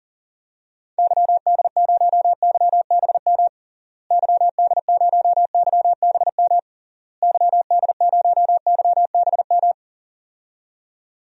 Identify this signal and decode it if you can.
Morse code, heard as a keyed tone, about 37 wpm, 700 Hz. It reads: YD0YBM YD0YBM YD0YBM
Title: DXCC-100 Call Signs - 6 Characters - Repeated 3x - 40wpm